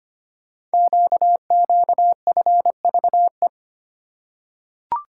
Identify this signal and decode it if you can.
Morse code, heard as a keyed tone, about 25 wpm, 700 Hz. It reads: QQFVE
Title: Code Groups - 5 Letter:Numbers - 25wpm